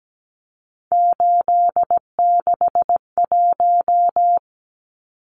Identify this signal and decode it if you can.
Morse code, heard as a keyed tone, about 17 wpm, 700 Hz. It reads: 861
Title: Numbers - 3-Digit 17wpm